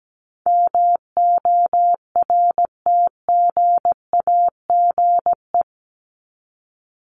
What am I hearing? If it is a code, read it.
Morse code, heard as a keyed tone, about 17 wpm, 700 Hz. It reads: MORTGAGE
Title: Top 1751-2000 Words 17wpm